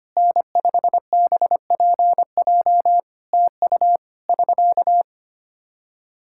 Morse code, heard as a keyed tone, about 25 wpm, 700 Hz. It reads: N5BPJ TU <SK>
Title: Practice QSO - Exchange 4 - Ending - 25wpm